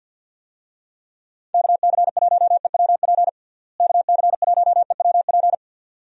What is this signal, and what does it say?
Morse code, heard as a keyed tone, about 48 wpm, 700 Hz. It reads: KC1EWP KC1EWP
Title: USA Call Signs - 6 Characters - Repeated 2x - 50wpm